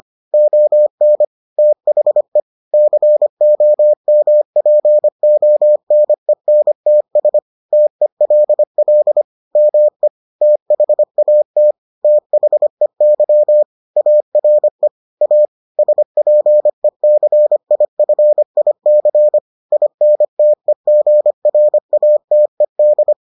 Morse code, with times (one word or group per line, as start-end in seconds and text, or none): 0.3-1.2 ON
1.6-2.4 THE
2.7-7.4 COMPONENTS
7.7-9.2 TELL
9.6-10.1 ME
10.4-11.7 THAT
12.0-13.6 THEY
14.0-14.9 ARE
15.2-15.4 A
15.8-19.4 SPECIFIC
19.7-23.1 INTEGRATED